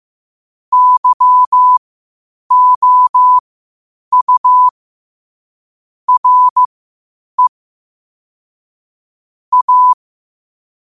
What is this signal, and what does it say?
Morse code, heard as a keyed tone, about 15 wpm, spaced as for 8 wpm, 1000 Hz. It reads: YOU RE A
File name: qotdfiles / 8wpm